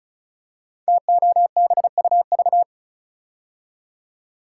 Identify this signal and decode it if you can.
Morse code, heard as a keyed tone, about 36 wpm, 700 Hz. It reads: TOBUV
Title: Code Groups - 5 Letters - 35wpm